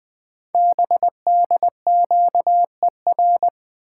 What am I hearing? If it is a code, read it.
Morse code, heard as a keyed tone, about 20 wpm, 700 Hz. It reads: BDQER
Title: Code Groups - 5 Letters - 20wpm